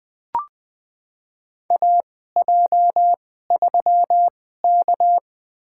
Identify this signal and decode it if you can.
Morse code, heard as a keyed tone, about 20 wpm, spaced as for 14 wpm, 700 Hz. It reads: AJ3K
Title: USA Call Signs - 4 Characters - 2x Character Spacing - 20wpm